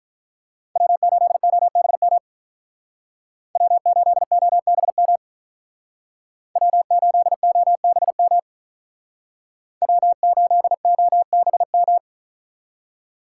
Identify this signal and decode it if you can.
Morse code, heard as a keyed tone, about 45 wpm, 700 Hz. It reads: W8OBM W8OBM W8OBM W8OBM